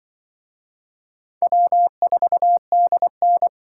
Morse code, heard as a keyed tone, about 25 wpm, 700 Hz. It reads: W4DN